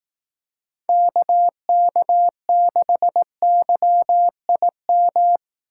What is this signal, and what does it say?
Morse code, heard as a keyed tone, about 18 wpm, 700 Hz. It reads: KK6YIM
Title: Speed Racing - USA 2x3 Call Signs 15wpm